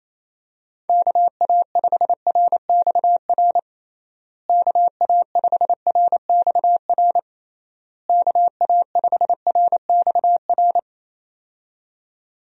Morse code, heard as a keyed tone, about 28 wpm, 700 Hz. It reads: KA5RXR KA5RXR KA5RXR
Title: USA Call Signs - 6 Characters - Repeated 3x - 28wpm